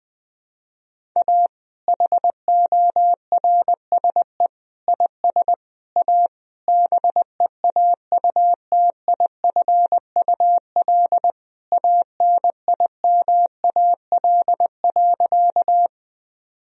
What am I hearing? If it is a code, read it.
Morse code, heard as a keyed tone, about 20 wpm, 700 Hz. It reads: A HORSE IS A BEAUTIFUL ANIMAL.
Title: Sentences from Top 1000 Words - Review Part 1 - 20wpm